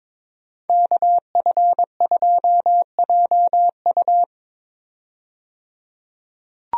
Morse code, heard as a keyed tone, about 22 wpm, 700 Hz. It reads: KF2JU
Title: USA Call Signs - 5 Characters - 22wpm